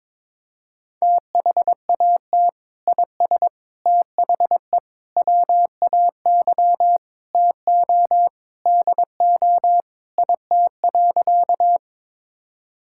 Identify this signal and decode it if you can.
Morse code, heard as a keyed tone, about 22 wpm, 700 Hz. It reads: THAT IS THE WAY TO DO IT.
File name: practice / 22wpm / Sentences with 3-Word Phrases - 22wpm